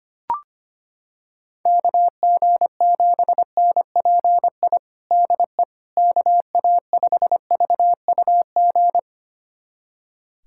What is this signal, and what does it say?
Morse code, heard as a keyed tone, about 25 wpm, 700 Hz. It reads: KG7NPI DE KA5VUG
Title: USA Callsign DE Callsign - 6 Characters - 25wpm